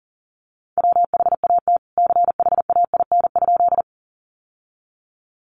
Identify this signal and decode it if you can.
Morse code, heard as a keyed tone, about 40 wpm, 700 Hz. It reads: WHAT CHAIN?
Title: Sentences from Top 1500 Words 40wpm